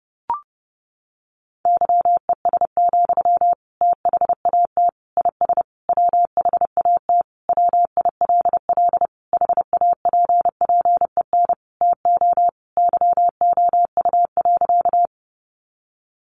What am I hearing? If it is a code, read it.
Morse code, heard as a keyed tone, about 30 wpm, 700 Hz. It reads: YES, THAT IS WHAT WILL HAPPEN TO YOU.
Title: Sentences from Top 500 Words 30wpm